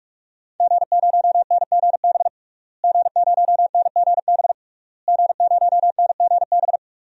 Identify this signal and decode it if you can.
Morse code, heard as a keyed tone, about 48 wpm, 700 Hz. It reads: G0NGB G0NGB G0NGB
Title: DXCC-100 Call Signs - 5 Characters - Repeated 3x - 45wpm